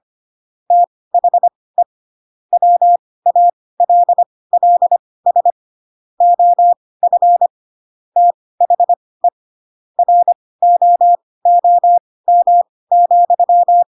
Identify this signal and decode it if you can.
Morse code, heard as a keyed tone, about 25 wpm, spaced as for 18 wpm, 700 Hz. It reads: THE WALLS OF THE ROOM,
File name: Pinocchio / Pinocchio ch1 12 25 ik1hge